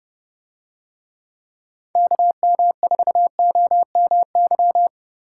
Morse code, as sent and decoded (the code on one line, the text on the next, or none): -.- -- ....- --- -- -.--
KM4OMY